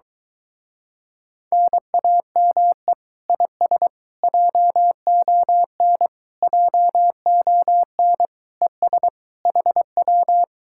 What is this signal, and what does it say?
Morse code, heard as a keyed tone, about 23 wpm, 700 Hz. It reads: NAME IS JON JON ES HW